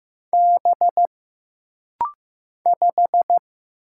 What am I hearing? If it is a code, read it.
Morse code, heard as a keyed tone, about 15 wpm, 700 Hz. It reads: B 5
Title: Single Letter-Number Alt - Rapid-Fire 15wpm